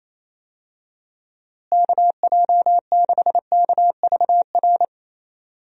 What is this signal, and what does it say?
Morse code, heard as a keyed tone, about 28 wpm, 700 Hz. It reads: KJ6KVR
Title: USA Call Signs - 6 Characters - Repeated 2x - 28wpm